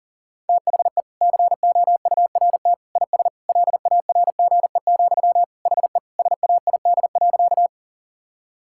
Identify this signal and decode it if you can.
Morse code, heard as a keyed tone, about 40 wpm, 700 Hz. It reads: THE COURT IS LARGE, HE SAID.
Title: Sentences from Top 1600 Words 40wpm